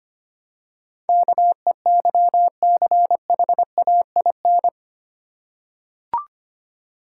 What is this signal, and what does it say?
Morse code, heard as a keyed tone, about 25 wpm, 700 Hz. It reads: KEYCHAIN
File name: practice / 25wpm / Difficult Words with Hints - 25wpm